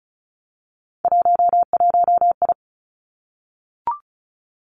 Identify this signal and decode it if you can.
Morse code, heard as a keyed tone, about 36 wpm, 700 Hz. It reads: 11I
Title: Three Challenging Letter-Number Combinations 35wpm